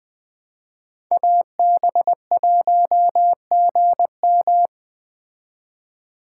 Morse code, heard as a keyed tone, about 20 wpm, 700 Hz. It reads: AB1GM
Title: Speed Racing - USA 2x2 Call Signs 17wpm